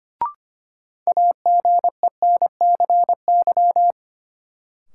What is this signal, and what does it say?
Morse code, heard as a keyed tone, about 25 wpm, 700 Hz. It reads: AGENCY
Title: Sentences from Top 1300 Words - New Word Drill Alt - 25wpm